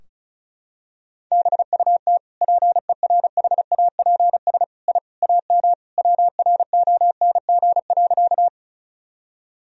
Morse code, heard as a keyed tone, about 36 wpm, 700 Hz. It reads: BUT PERHAPS I AM WRONG.
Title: Sentences from Top 1000 Words - Review Part 2 - 35wpm